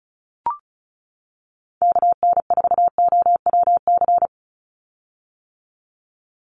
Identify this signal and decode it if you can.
Morse code, heard as a keyed tone, about 36 wpm, 700 Hz. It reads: KN4OWC
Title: USA Call Signs 35wpm